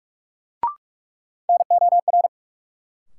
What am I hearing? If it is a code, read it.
Morse code, heard as a keyed tone, about 43 wpm, 700 Hz. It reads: NOR
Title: Sentences from Top 1500 Words - New Word Drill - 45wpm